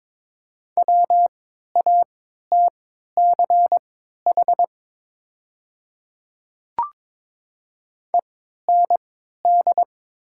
Morse code, heard as a keyed tone, about 23 wpm, spaced as for 13 wpm, 700 Hz. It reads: WATCH END
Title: 3x Character Spacing - Top 300 Words - 22wpm